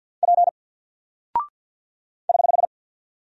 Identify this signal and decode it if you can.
Morse code, heard as a keyed tone, about 48 wpm, 700 Hz. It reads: P <HH>
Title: Single Character (Letter:Number:Punctuation:Prosign) 50wpm